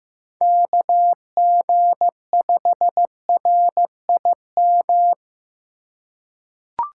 Morse code, heard as a keyed tone, about 15 wpm, 700 Hz. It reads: KG5RIM
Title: USA Call Signs - 6 Characters - 4x Character Spacing - 15wpm